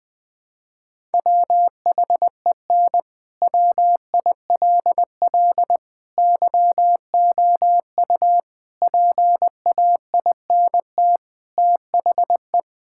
Morse code, heard as a keyed tone, about 20 wpm, 700 Hz. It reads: WHEN WILL YOU PAINT THE
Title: Sentences from Top 1000 Words - Review Part 6 - 20wpm